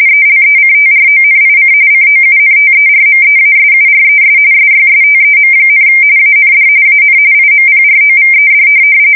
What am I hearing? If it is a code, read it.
RTTY, about 45.45 baud. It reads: ELECTRONIC COUNTERMEASURES SYSTEM, AND AN IMPROVED
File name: fsk170 / USS-New-Jersey